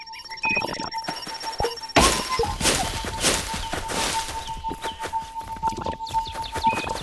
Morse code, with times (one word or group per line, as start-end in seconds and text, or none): none